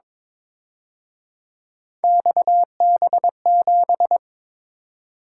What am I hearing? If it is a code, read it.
Morse code, heard as a keyed tone, about 27 wpm, 700 Hz. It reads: XB7
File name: Speed-Racing - Three Letter-Number Combinations - 22wpm